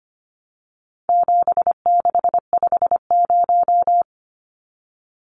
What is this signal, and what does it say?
Morse code, heard as a keyed tone, about 25 wpm, 700 Hz. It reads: 7650
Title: Numbers - 4-Digit 25wpm